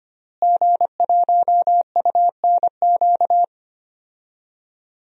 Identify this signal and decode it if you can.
Morse code, heard as a keyed tone, about 25 wpm, 700 Hz. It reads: G1UNQ
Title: DXCC-100 Call Signs - 5 Characters - Repeated 3x - 25wpm